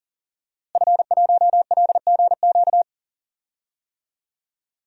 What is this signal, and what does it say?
Morse code, heard as a keyed tone, about 40 wpm, 700 Hz. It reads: F1LGQ
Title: DXCC-100 Call Signs - 5 Characters - Repeated 2x - 40wpm